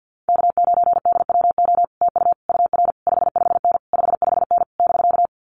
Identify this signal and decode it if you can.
Morse code, heard as a keyed tone, about 50 wpm, 700 Hz. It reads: K9DWO TU UR 55N 55N <BK>
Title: Common QSO Phrases Encore 50wpm